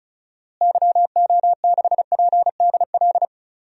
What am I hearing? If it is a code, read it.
Morse code, heard as a keyed tone, about 36 wpm, 700 Hz. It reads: YO6PDL
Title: DXCC-100 Call Signs - 6 Characters - Repeated 2x - 35wpm